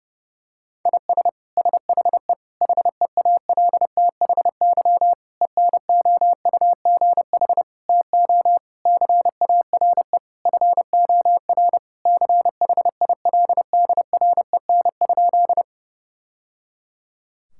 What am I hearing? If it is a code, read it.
Morse code, heard as a keyed tone, about 32 wpm, 700 Hz. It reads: IS SHE HEALTHY ENOUGH TO CARE FOR CHILDREN?
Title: Sentences from Top 1000 Words - Review Part 5 - 30wpm